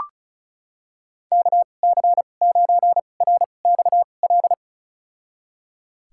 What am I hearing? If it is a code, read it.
Morse code, heard as a keyed tone, about 34 wpm, spaced as for 25 wpm, 700 Hz. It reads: KC9RXL
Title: USA Call Signs - 6 Characters - 2x Character Spacing - 35wpm